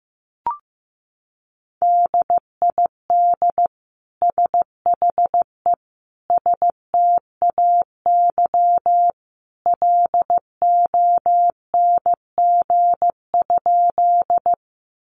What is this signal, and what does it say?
Morse code, heard as a keyed tone, about 15 wpm, 700 Hz. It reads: DID SHE STAY LONG?
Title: Sentences from Top 500 Words 15wpm